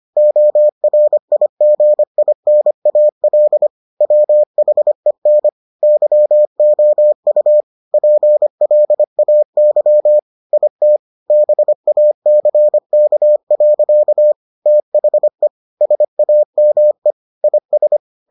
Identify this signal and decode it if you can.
Morse code, heard as a keyed tone, about 25 wpm, 600 Hz. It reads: ORIGINAL WHEN YOU PLAY IT BACK. THE SAME IS